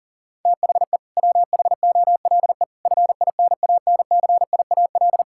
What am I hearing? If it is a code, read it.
Morse code, heard as a keyed tone, about 40 wpm, 700 Hz. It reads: THE WHOLE FINANCIAL
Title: Sentences from Top 1000 Words - Review Part 5 - 40wpm